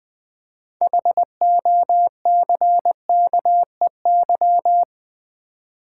Morse code, heard as a keyed tone, about 25 wpm, 700 Hz. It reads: HOCKEY
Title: Easy Words - Speed-Racing with Hints - 20wpm